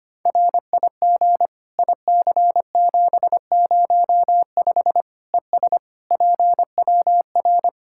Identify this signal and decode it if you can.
Morse code, heard as a keyed tone, about 25 wpm, 700 Hz. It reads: RIG IC705 ES PWR
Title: Practice QSO - Exchange 2 - Info - 25wpm